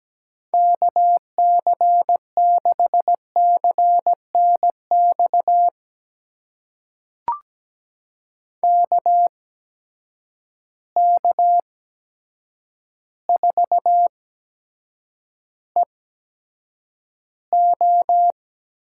Morse code, heard as a keyed tone, about 17 wpm, spaced as for 8 wpm, 700 Hz. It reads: KC6CNX KK4EO
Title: USA Call Signs - 6 Characters - 8x Character Spacing - 17wpm